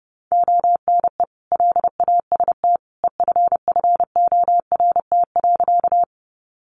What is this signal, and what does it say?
Morse code, heard as a keyed tone, about 30 wpm, 700 Hz. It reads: ONE LAST EFFORT.
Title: Sentences from Top 1200 Words 30wpm